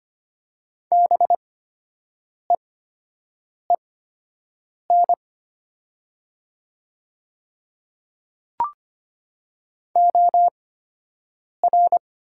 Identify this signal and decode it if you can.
Morse code, heard as a keyed tone, about 25 wpm, spaced as for 7 wpm, 700 Hz. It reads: BEEN OR
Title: 8x Character Spacing - Top 100 Words - 25wpm